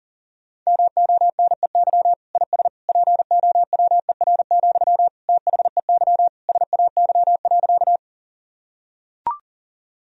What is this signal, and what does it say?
Morse code, heard as a keyed tone, about 40 wpm, 700 Hz. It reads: MONEY IS POWER, THEY SAY.
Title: Sentences from Top 1000 Words - Review Part 2 - 40wpm